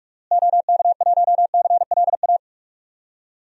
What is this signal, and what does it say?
Morse code, heard as a keyed tone, about 42 wpm, 700 Hz. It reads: OK1CLA
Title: DXCC-100 Call Signs - 6 Characters - Repeated 3x - 45wpm